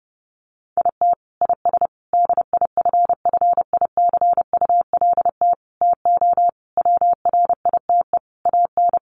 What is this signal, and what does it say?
Morse code, heard as a keyed tone, about 30 wpm, 700 Hz. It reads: IT IS DIFFICULT TO WRITE AN
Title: Sentences from Top 1500 Words 30wpm